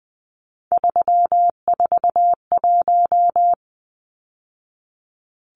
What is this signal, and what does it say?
Morse code, heard as a keyed tone, about 20 wpm, 700 Hz. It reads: 341
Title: Numbers - 3-Digit 20wpm